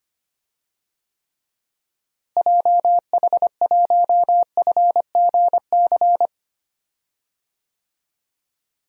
Morse code, heard as a keyed tone, about 25 wpm, 700 Hz. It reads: JH1FGC
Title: DXCC-100 Call Signs - 6 Characters - Repeated 3x - 25wpm